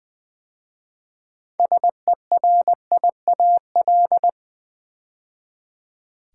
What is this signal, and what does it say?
Morse code, heard as a keyed tone, about 20 wpm, 700 Hz. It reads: SERIAL
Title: 6 Letter Words - First 3 Letters Given - 20wpm